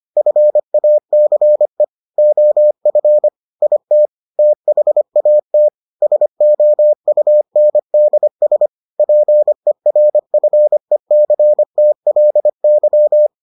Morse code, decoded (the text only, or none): FACE OF IT THAT SOUNDS PERFECTLY